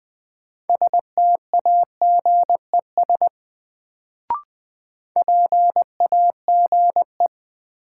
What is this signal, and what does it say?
Morse code, heard as a keyed tone, about 20 wpm, 700 Hz. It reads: STAGES PAGE
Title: Sets of 5 Words Encore - New Word Drill - 20wpm